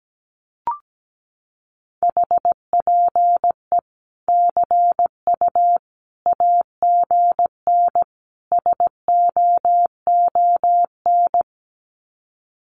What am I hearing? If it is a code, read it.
Morse code, heard as a keyed tone, about 17 wpm, 700 Hz. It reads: HPE CU AGN SOON
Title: Common QSO Phrases 17wpm